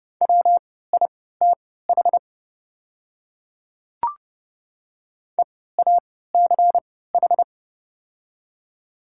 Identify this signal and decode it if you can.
Morse code, heard as a keyed tone, about 30 wpm, spaced as for 17 wpm, 700 Hz. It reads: WITH EACH
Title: 3x Character Spacing - Top 100 Words - 30wpm